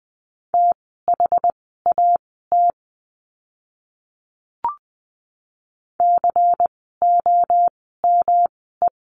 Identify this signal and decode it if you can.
Morse code, heard as a keyed tone, about 20 wpm, spaced as for 14 wpm, 700 Hz. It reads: THAT COME
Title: 146 Lesson 24f - Letter y - Words 20wpm